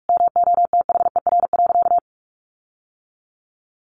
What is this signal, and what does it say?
Morse code, heard as a keyed tone, about 44 wpm, 700 Hz. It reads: MOTHER.
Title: Sentences from Top 1500 Words 45wpm